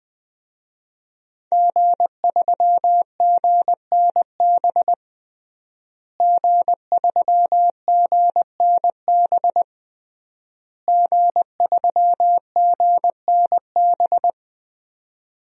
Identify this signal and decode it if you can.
Morse code, heard as a keyed tone, about 20 wpm, 700 Hz. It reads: G3GNB G3GNB G3GNB
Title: DXCC-100 Call Signs - 5 Characters - Repeated 3x - 20wpm